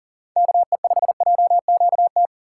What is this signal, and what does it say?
Morse code, heard as a keyed tone, about 40 wpm, 700 Hz. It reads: KE5JQT
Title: USA Call Signs - 6 Characters - 3x Character Spacing - 40wpm